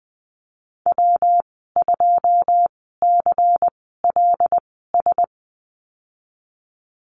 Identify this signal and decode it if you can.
Morse code, heard as a keyed tone, about 20 wpm, spaced as for 14 wpm, 700 Hz. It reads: W2CLS
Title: 147 Lesson 24g - Letter y - Call Signs 20wpm